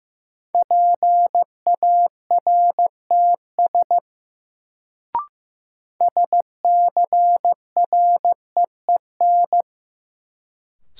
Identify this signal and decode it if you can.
Morse code, heard as a keyed tone, about 15 wpm, 700 Hz. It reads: PARTS SCREEN
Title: Sentences from Top 1000 Words - Review Part 6 - New Word Drill - 15wpm